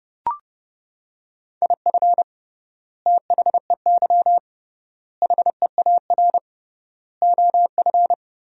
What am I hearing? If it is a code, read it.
Morse code, heard as a keyed tone, about 30 wpm, 700 Hz. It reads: IF THEY HEAR OF
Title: 3x Word-Spacing - Sentences from Top 400 Words - 30wpm